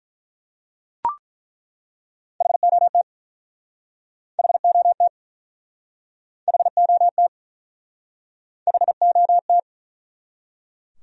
Morse code, heard as a keyed tone, about 48 wpm, 700 Hz. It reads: HOT HOT HOT HOT